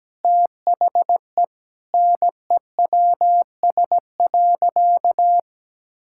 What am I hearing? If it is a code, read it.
Morse code, heard as a keyed tone, about 17 wpm, 700 Hz. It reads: THE NEWS.
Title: Sentences from Top 1000 Words - Review Part 6 - 17wpm